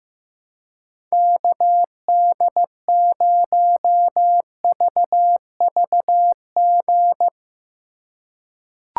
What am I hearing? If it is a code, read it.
Morse code, heard as a keyed tone, about 15 wpm, 700 Hz. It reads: KD0VVG
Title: USA Call Signs - 6 Characters - 2x Character Spacing - 15wpm